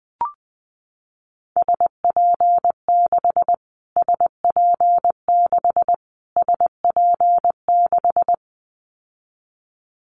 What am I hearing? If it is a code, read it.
Morse code, heard as a keyed tone, about 20 wpm, 700 Hz. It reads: SP6 SP6 SP6
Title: Callsign Prefixes for Top 25 DXCC - Repeated 3x - 20wpm